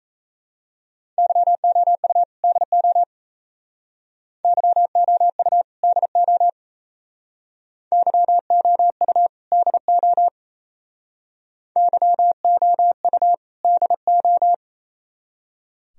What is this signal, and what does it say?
Morse code, heard as a keyed tone, about 35 wpm, 700 Hz. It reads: YOU DO YOU DO YOU DO YOU DO
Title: Speed-Racing - Sets of 2 Words 28wpm